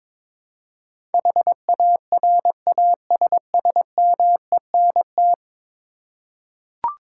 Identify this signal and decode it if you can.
Morse code, heard as a keyed tone, about 21 wpm, 700 Hz. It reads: HARASSMENT
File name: practice / 22wpm / 10 Letter Words - First 3 Letters Given - 22wpm